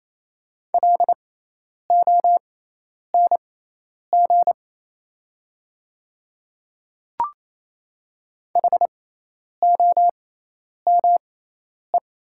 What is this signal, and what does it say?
Morse code, heard as a keyed tone, about 28 wpm, spaced as for 10 wpm, 700 Hz. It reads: LONG HOME
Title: 6x Character Spacing - Top 200 Words - 28wpm